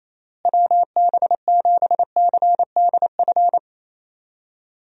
Morse code, heard as a keyed tone, about 28 wpm, 700 Hz. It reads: WB7CDF
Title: USA Call Signs - 6 Characters - Repeated 3x - 28wpm